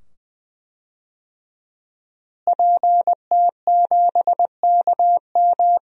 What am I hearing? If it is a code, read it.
Morse code, heard as a keyed tone, about 20 wpm, 700 Hz. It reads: PT7KM
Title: DXCC-100 Call Signs - 5 Characters - Repeated 2x - 20wpm